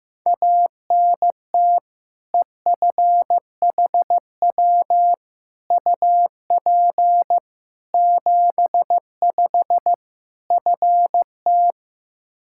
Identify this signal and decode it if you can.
Morse code, heard as a keyed tone, about 15 wpm, 700 Hz. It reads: ANT EFHW UP 75 FT
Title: Practice QSO - Exchange 2 - Info - 15wpm